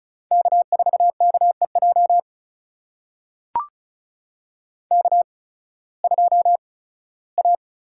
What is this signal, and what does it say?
Morse code, heard as a keyed tone, about 35 wpm, spaced as for 18 wpm, 690 Hz. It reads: K4KEJ K2A
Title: USA Call Signs - 5 Characters - 8x Character Spacing - 35wpm